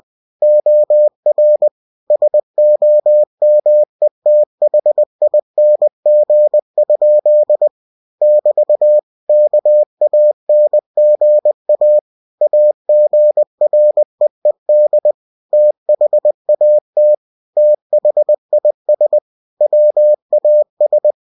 Morse code, heard as a keyed tone, about 20 wpm, 600 Hz. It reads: OR SOMETHING? <BT> KANGA AGREED THAT THIS WAS
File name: cw-books / winnie-the-pooh / winnie-20wpm-page-0074